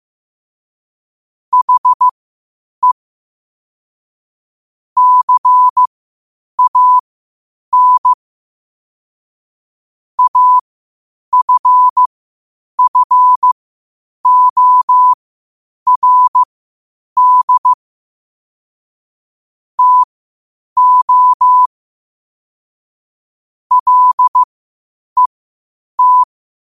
Morse code, heard as a keyed tone, about 15 wpm, spaced as for 8 wpm, 1000 Hz. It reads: HE CAN AFFORD TO LET
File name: qotdfiles / 8wpm